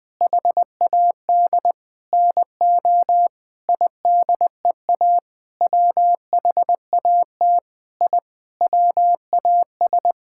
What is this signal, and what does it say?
Morse code, heard as a keyed tone, about 20 wpm, 700 Hz. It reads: HAD NO IDEA WHAT I WAS
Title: Sentences with 8-Word Phrases - 20wpm